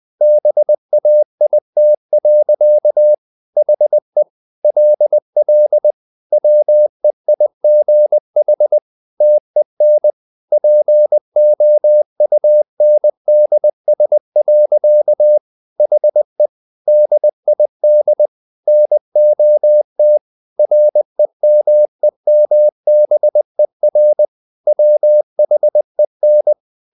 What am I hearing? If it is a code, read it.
Morse code, heard as a keyed tone, about 20 wpm, 600 Hz. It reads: BAIT. HE LL WEIGH TEN POUNDS. HE DID NOT REMEMBER WHEN
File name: the old man and the sea / the old man and the sea part 1 20wpm